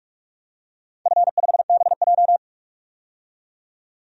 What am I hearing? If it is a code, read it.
Morse code, heard as a keyed tone, about 42 wpm, 700 Hz. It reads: F5BJ
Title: DXCC-100 Call Signs - 4 Characters - 45wpm